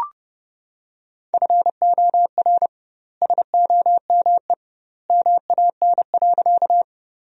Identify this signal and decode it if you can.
Morse code, heard as a keyed tone, about 30 wpm, 700 Hz. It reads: FOR SOME MAN.
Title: 2x Word-Spacing - Sentences from Top 200 Words - 30wpm